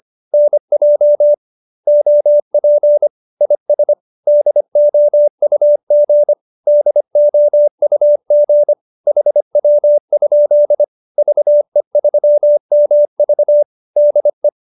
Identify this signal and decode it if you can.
Morse code, heard as a keyed tone, about 25 wpm, 590 Hz. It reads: NJ OP IS DOUG DOUG HW? VE3MV DE